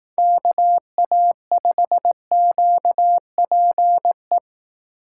Morse code, heard as a keyed tone, about 18 wpm, 700 Hz. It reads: KA5QPE
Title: Speed Racing - USA 2x3 Call Signs 15wpm